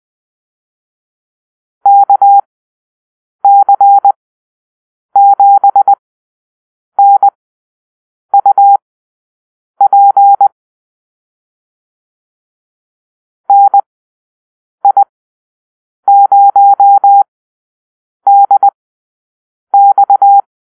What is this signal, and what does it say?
Morse code, heard as a keyed tone, about 20 wpm, spaced as for 7 wpm, 800 Hz. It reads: KC7NUP NI0DX
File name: arrl / Call-Sign Drill-8